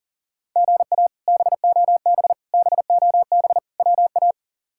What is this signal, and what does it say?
Morse code, heard as a keyed tone, about 37 wpm, 690 Hz. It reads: GA BOB BOB WA
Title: K1USN SST QSO Practice - 40wpm